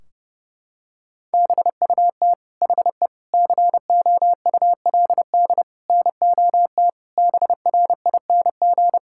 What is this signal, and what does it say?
Morse code, heard as a keyed tone, about 30 wpm, 700 Hz. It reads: BUT HE COULD NOT BRING
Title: Sentences from Top 1600 Words 30wpm